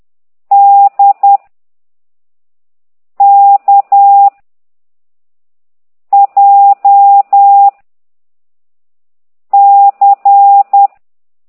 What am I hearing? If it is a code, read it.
Morse code, heard as a keyed tone, about 10 wpm, 800 Hz. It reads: D K J C
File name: exerc7